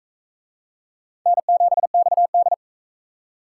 Morse code, heard as a keyed tone, about 41 wpm, 700 Hz. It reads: N7XD